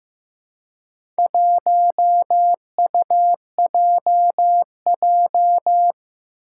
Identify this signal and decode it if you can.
Morse code, heard as a keyed tone, about 15 wpm, 700 Hz. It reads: 1UJJ
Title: Four Challenging Letter-Number Combinations 15wpm